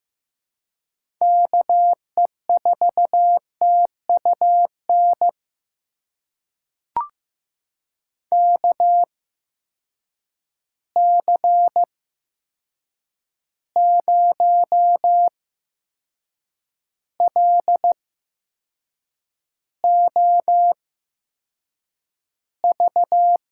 Morse code, heard as a keyed tone, about 15 wpm, spaced as for 6 wpm, 700 Hz. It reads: KE4TUN KC0LOV
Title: USA Call Signs - 6 Characters - 8x Character Spacing - 15wpm